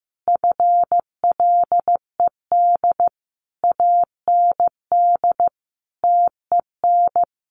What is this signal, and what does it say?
Morse code, heard as a keyed tone, about 15 wpm, 700 Hz. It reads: FLED AND TEN
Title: A Sound Of Thunder 15wpm